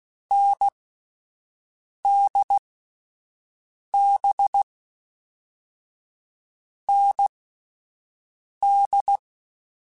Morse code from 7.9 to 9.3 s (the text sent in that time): D